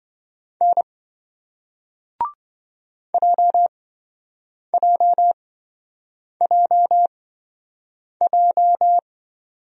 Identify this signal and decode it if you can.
Morse code, heard as a keyed tone, about 27 wpm, spaced as for 7 wpm, 700 Hz. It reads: N JJJJ